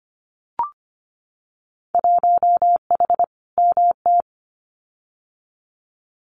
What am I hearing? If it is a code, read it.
Morse code, heard as a keyed tone, about 25 wpm, 700 Hz. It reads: 1H MT
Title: Winter Field Day Exchanges Encore 25wpm